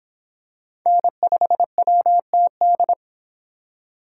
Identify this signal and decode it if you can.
Morse code, heard as a keyed tone, about 26 wpm, 700 Hz. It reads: N5WTD